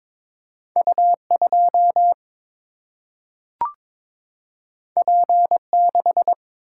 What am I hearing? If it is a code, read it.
Morse code, heard as a keyed tone, about 22 wpm, 700 Hz. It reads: U2 P6
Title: Single-Letter Single-Number 22wpm